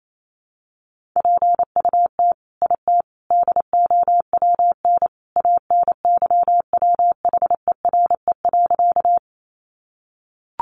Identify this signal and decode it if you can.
Morse code, heard as a keyed tone, about 28 wpm, 700 Hz. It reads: PUT IT DOWN ANYWHERE.
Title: Sentences from Top 1400 Words 28wpm